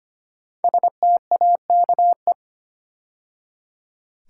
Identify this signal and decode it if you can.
Morse code, heard as a keyed tone, about 25 wpm, 700 Hz. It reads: STAKE
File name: 5 Letter Words - First 3 Letters Given - 25wpm